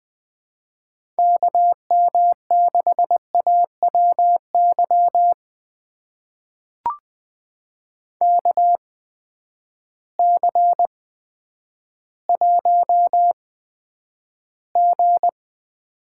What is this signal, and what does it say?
Morse code, heard as a keyed tone, about 20 wpm, spaced as for 10 wpm, 700 Hz. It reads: KM6AWY KC1G
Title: USA Call Signs - 6 Characters - 8x Character Spacing - 20wpm